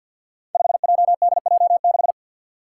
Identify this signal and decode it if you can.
Morse code, heard as a keyed tone, about 51 wpm, 700 Hz. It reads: 5JDJ6
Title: Five Challenging Letter-Number Combinations 50wpm